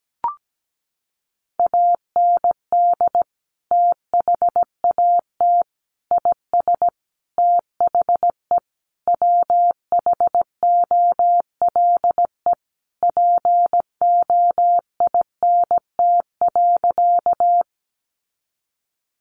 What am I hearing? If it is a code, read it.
Morse code, heard as a keyed tone, about 17 wpm, 700 Hz. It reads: AND THAT IS THE WHOLE POINT.
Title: Sentences from Top 400 Words 17wpm